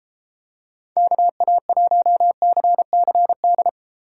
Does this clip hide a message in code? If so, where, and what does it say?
Yes: Morse code from 1.0 to 3.7 s, heard as a keyed tone, about 33 wpm, 700 Hz. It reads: KA1CCD